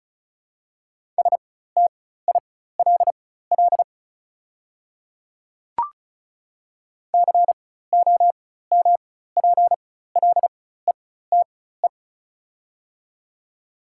Morse code, heard as a keyed tone, about 36 wpm, spaced as for 16 wpm, 690 Hz. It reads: STILL COMPLETE
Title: 4x Character Spacing - Top 500 Words - 35wpm